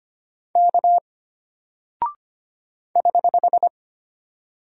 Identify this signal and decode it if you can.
Morse code, heard as a keyed tone, about 25 wpm, 700 Hz. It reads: K <HH>
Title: Single Character (Letter:Number:Punctuation:Prosign) Alt 25wpm